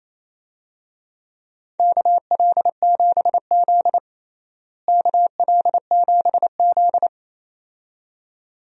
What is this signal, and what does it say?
Morse code, heard as a keyed tone, about 28 wpm, 700 Hz. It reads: KL7Z KL7Z
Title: DXCC-100 Call Signs - 4 Characters - Repeated 2x - 28wpm